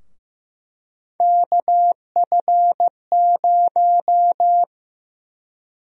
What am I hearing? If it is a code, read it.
Morse code, heard as a keyed tone, about 15 wpm, 700 Hz. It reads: KF0
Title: USA Call Sign Prefixes - 15wpm